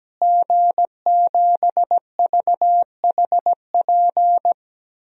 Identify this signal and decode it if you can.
Morse code, heard as a keyed tone, about 17 wpm, 700 Hz. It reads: G7VHP
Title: DXCC-100 Call Signs - 5 Characters - Repeated 2x - 17wpm